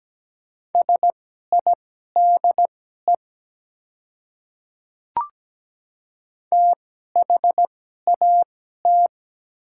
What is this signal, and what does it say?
Morse code, heard as a keyed tone, about 17 wpm, spaced as for 12 wpm, 700 Hz. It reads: SIDE THAT
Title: 2x Character Spacing - Top 200 Words - 17wpm